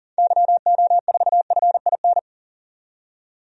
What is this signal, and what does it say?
Morse code, heard as a keyed tone, about 40 wpm, 700 Hz. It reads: YO4FIN
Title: DXCC-100 Call Signs - 6 Characters - 40wpm